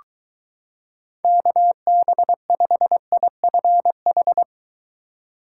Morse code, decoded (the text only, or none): KB5IFH